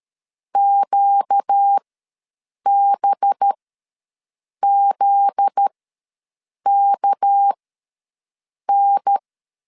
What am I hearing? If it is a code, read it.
Morse code, heard as a keyed tone, about 13 wpm, spaced as for 6 wpm, 800 Hz. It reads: QBZKN
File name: LEZIONI / 10 SIGNS / 10B INT ws 12wpm 3527